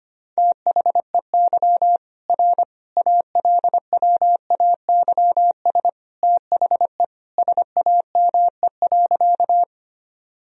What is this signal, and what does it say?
Morse code, heard as a keyed tone, about 25 wpm, 700 Hz. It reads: THEY R ALWAYS THE SAME.
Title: Sentences from Top 300 Words and Abbreviations 25wpm